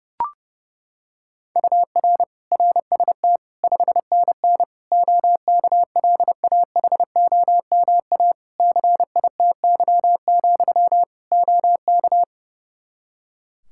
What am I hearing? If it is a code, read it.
Morse code, heard as a keyed tone, about 30 wpm, 700 Hz. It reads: UR RST 5NN OKLAHOMA CITY, OK
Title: Practice QSO - Exchange 1 - Beginning - 30wpm